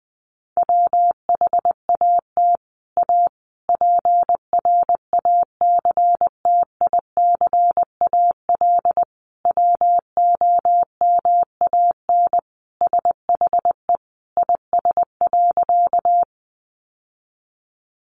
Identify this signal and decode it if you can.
Morse code, heard as a keyed tone, about 20 wpm, 700 Hz. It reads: WHAT A PRACTICAL WOMAN SHE IS.
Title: Sentences from Top 1400 Words 20wpm